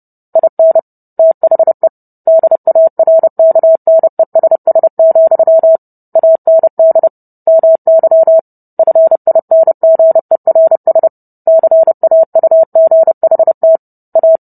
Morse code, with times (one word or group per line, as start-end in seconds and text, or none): 0.4-0.8 IN
1.2-1.9 THE
2.3-5.8 DARKNESS,
6.2-7.1 AND
7.5-8.4 MY
8.8-11.1 FINGERS
11.5-13.8 CAUGHT
14.2-14.4 A